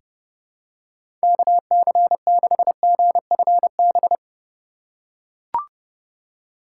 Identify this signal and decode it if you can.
Morse code, heard as a keyed tone, about 32 wpm, 700 Hz. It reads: KC6GFB
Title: USA Call Signs - 6 Characters - 2x Character Spacing - 30wpm